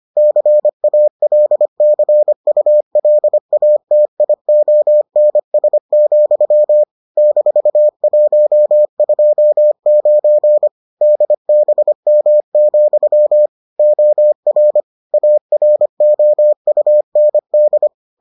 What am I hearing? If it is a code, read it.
Morse code, heard as a keyed tone, about 25 wpm, 600 Hz. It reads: CALCULATIONS, -129 DBM, OR AROUND